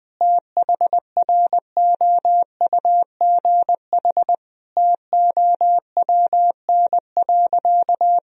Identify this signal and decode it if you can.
Morse code, heard as a keyed tone, about 21 wpm, 700 Hz. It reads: THROUGH TOWN.
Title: Sentences from Top 1000 Words - Review Part 2 - 20wpm